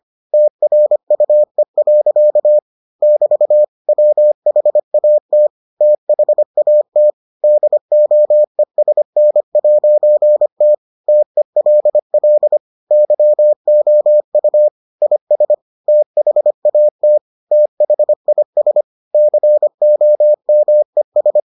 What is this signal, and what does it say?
Morse code, heard as a keyed tone, about 25 wpm, 600 Hz. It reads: TRUE. <BT> WHAT THAT DOESN'T TELL YOU IS THAT THIS COMES